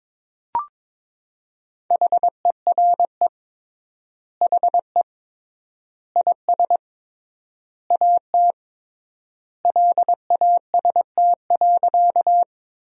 Morse code, heard as a keyed tone, about 21 wpm, 700 Hz. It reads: HERE HE IS AT LAST.
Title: 3x Word-Spacing - Sentences from Top 300 Words - 22wpm